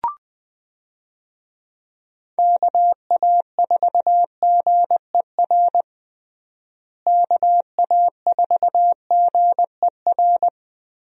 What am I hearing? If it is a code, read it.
Morse code, heard as a keyed tone, about 20 wpm, 700 Hz. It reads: KA4GER KA4GER
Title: USA Call Signs - 6 Characters - Repeated 2x - 20wpm